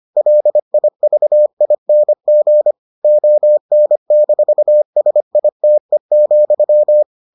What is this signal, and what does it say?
Morse code, heard as a keyed tone, about 25 wpm, 600 Hz. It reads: LIVING ON-SITE,